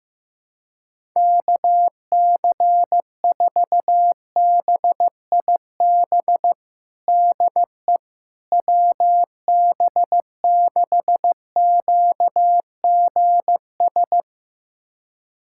Morse code, heard as a keyed tone, about 15 wpm, 700 Hz. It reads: KC4BIB DE WB6QGS
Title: USA Callsign DE Callsign - 6 Characters - 15wpm